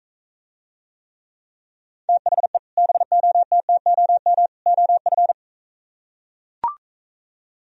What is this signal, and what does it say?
Morse code, heard as a keyed tone, about 44 wpm, 700 Hz. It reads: THE BOTTOM OF